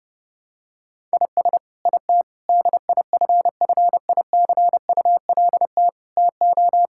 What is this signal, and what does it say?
Morse code, heard as a keyed tone, about 30 wpm, 700 Hz. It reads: IS IT DIFFICULT TO
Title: Sentences from Top 1000 Words - Review Part 6 - 30wpm